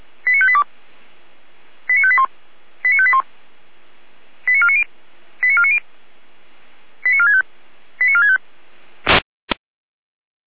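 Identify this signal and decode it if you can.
Five-tone selective call (ZVEI), heard as quick runs of five tones, tones of about 70 ms. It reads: 89582 89581 89581 89490 89490 89465 89465
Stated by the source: ZVEI five-tone call 89531 received by SDR.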